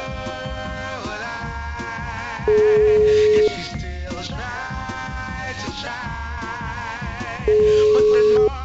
Ringback tone (Europe, 425 Hz), with noise.